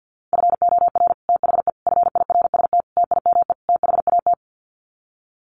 Morse code, heard as a keyed tone, about 50 wpm, 700 Hz. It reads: FOR THE FIRST TIME THAT
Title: Sets of 5 Words Encore 50wpm